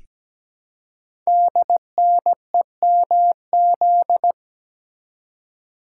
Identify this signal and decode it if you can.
Morse code, heard as a keyed tone, about 17 wpm, 700 Hz. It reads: DNEMZ
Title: Code Groups - 5 Letters - 17wpm